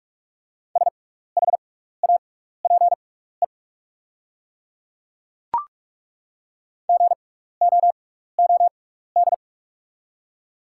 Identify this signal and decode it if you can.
Morse code, heard as a keyed tone, about 43 wpm, spaced as for 15 wpm, 700 Hz. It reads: SHAPE GOOD
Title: 6x Character Spacing - Top 500 Words - 45wpm